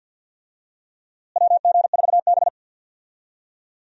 Morse code, heard as a keyed tone, about 51 wpm, 700 Hz. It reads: WG4B
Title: USA Call Signs - 4 Characters - Repeated 2x - 50wpm